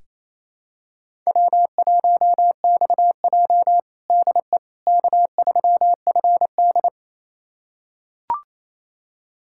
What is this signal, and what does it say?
Morse code, heard as a keyed tone, about 27 wpm, 700 Hz. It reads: W1XJ DE K3FD
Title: USA Callsign DE Callsign - 4 Characters - 28wpm